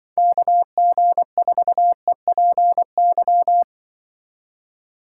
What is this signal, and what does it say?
Morse code, heard as a keyed tone, about 24 wpm, 700 Hz. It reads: KG4EPY
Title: Speed Racing - USA 2x3 Call Signs 20wpm